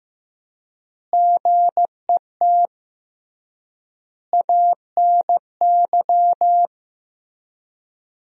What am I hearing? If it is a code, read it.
Morse code, heard as a keyed tone, about 15 wpm, 700 Hz. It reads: GET ANY
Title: 3x Word-Spacing - Sentences from Top 400 Words - 15wpm